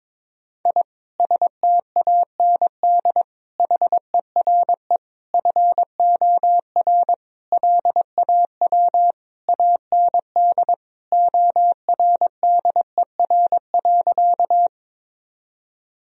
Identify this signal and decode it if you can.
Morse code, heard as a keyed tone, about 22 wpm, 700 Hz. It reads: I STAND HERE FOR LAW AND ORDER.
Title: Sentences from Top 1000 Words - Review Part 3 - 22wpm